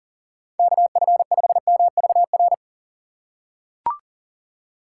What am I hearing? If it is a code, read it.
Morse code, heard as a keyed tone, about 40 wpm, 700 Hz. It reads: KF5MVR
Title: USA Call Signs - 6 Characters - 3x Character Spacing - 40wpm